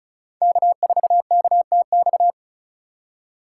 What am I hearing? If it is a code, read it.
Morse code, heard as a keyed tone, about 36 wpm, 700 Hz. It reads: K4KTX
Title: USA Call Signs - 2x Character Spacing - 35wpm